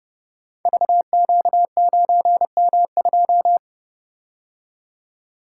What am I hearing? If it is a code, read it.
Morse code, heard as a keyed tone, about 30 wpm, 700 Hz. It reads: VQ9M2
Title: Code Groups - 5 Letter:Numbers - 30wpm